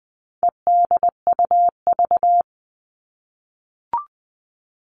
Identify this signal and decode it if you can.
Morse code, heard as a keyed tone, about 20 wpm, 700 Hz. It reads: EDUV
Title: All Cut Numbers - 4-Digit 20wpm